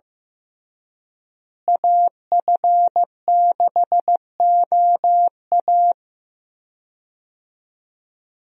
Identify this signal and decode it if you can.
Morse code, heard as a keyed tone, about 15 wpm, 700 Hz. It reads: AF6OA
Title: USA Call Signs - 5 Characters - Repeated 2x - 15wpm